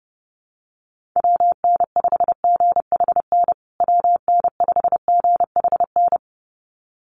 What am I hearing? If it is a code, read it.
Morse code, heard as a keyed tone, about 30 wpm, 700 Hz. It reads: WN5GHN WN5GHN
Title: USA Call Signs Repeated 2x 30wpm